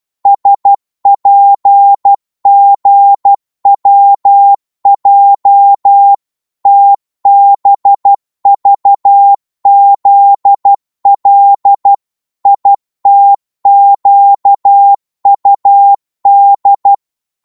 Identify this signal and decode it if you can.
Morse code, heard as a keyed tone, about 12 wpm, 800 Hz. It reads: SPGWJ TBVZL ITQUD
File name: Morse / ltr-12wpm